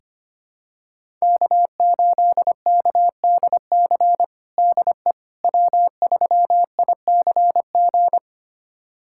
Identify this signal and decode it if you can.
Morse code, heard as a keyed tone, about 25 wpm, 700 Hz. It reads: K8KDC DE W3ICG
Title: USA Callsign DE Callsign - 5 Characters - 25wpm